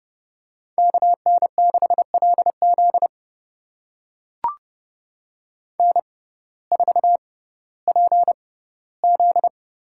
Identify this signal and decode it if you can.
Morse code, heard as a keyed tone, about 31 wpm, spaced as for 17 wpm, 700 Hz. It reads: KN6LZ N4PZ
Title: USA Call Signs - 5 Characters - 6x Character Spacing - 30wpm